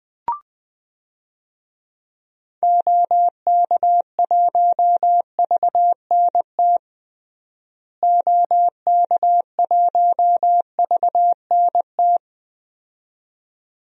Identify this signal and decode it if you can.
Morse code, heard as a keyed tone, about 20 wpm, 700 Hz. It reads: OK1VNT OK1VNT
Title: DXCC-100 Call Signs - 6 Characters - Repeated 2x - 20wpm